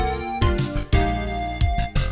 There is no code.